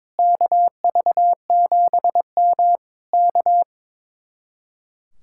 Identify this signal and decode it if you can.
Morse code, heard as a keyed tone, about 22 wpm, 700 Hz. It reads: KV7M K
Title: Speed-Racing Common QSO Phrases - 22wpm